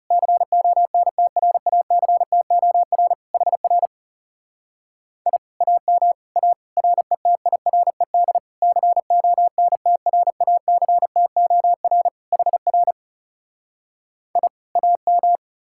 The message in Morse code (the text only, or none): CONTRACTOR HR I AM A RETIRED CONTRACTOR HR I AM